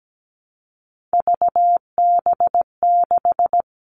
Morse code, heard as a keyed tone, about 17 wpm, 700 Hz. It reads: VB6
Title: Three Challenging Letter-Number Combinations 17wpm